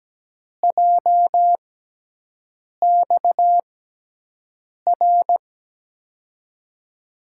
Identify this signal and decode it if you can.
Morse code, heard as a keyed tone, about 17 wpm, spaced as for 6 wpm, 700 Hz. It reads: JXR
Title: USA Call Signs - 5 Characters - 6x Character Spacing - 17wpm